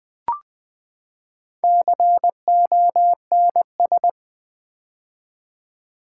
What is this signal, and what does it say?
Morse code, heard as a keyed tone, about 20 wpm, 700 Hz. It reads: CONS
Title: Build A Word - 4 Letters At A Time - 11-Letter Words - 20wpm